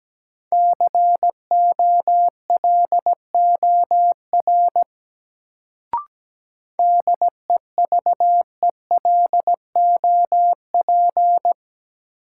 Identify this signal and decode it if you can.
Morse code, heard as a keyed tone, about 17 wpm, 700 Hz. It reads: COLOR DEVELOP
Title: Sentences from Top 1000 Words - Review Part 2 - New Word Drill - 17wpm